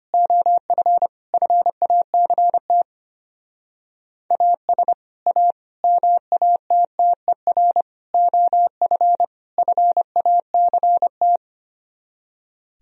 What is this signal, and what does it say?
Morse code, heard as a keyed tone, about 26 wpm, 700 Hz. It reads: OF FACT AS A MATTER OF FACT
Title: Speed-Racing - Sets of 5 Words 25wpm